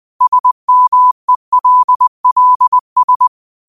Morse code, heard as a keyed tone, about 20 wpm, 1000 Hz. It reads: SMELLS
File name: qotdfiles / 20wpm